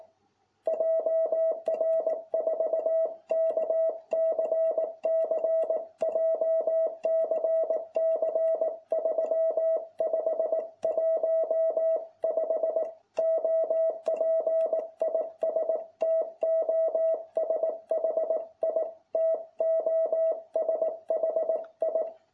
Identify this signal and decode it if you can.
Morse code, heard as a keyed tone, about 18 wpm, 660 Hz. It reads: JR4KCC JCC 3515 OP IS TOSHI TOSHI